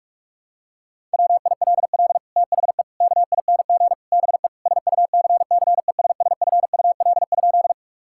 Morse code, heard as a keyed tone, about 41 wpm, 700 Hz. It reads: WILL THE KING BE SUCCESSFUL?
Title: Sentences from Top 1000 Words - Review Part 6 - 45wpm